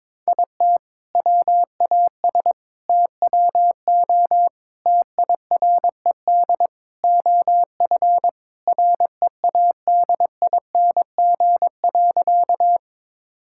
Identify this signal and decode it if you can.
Morse code, heard as a keyed tone, about 22 wpm, 700 Hz. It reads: IT WAS TWO TIRED OF READING.
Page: morse-code-ninja-practice-22wpm